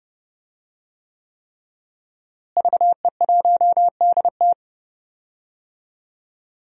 Morse code, heard as a keyed tone, about 30 wpm, 700 Hz. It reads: VE1DT
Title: DXCC-100 Call Signs - 5 Characters - Repeated 3x - 30wpm